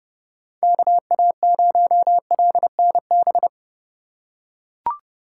Morse code, heard as a keyed tone, about 27 wpm, 700 Hz. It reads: KA0LNB